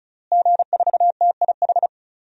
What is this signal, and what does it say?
Morse code, heard as a keyed tone, about 34 wpm, 700 Hz. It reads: G4TIH
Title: DXCC-100 Call Signs - 5 Characters - Repeated 2x - 35wpm